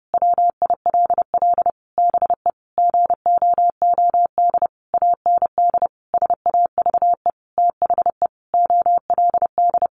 Morse code, heard as a keyed tone, about 30 wpm, 700 Hz. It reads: WILL BE GOOD AND SAVE THE OLD
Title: Sentences from Top 1100 Words 30wpm